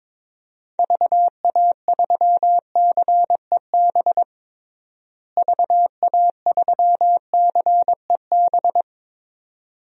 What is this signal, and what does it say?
Morse code, heard as a keyed tone, about 22 wpm, 700 Hz. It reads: VA3CEB VA3CEB
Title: DXCC-100 Call Signs - 6 Characters - Repeated 2x - 22wpm